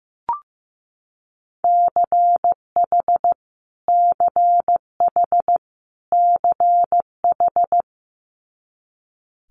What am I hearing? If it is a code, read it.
Morse code, heard as a keyed tone, about 15 wpm, 700 Hz. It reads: CH CH CH
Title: Multi-letter Phonemes Repeated 3x 15wpm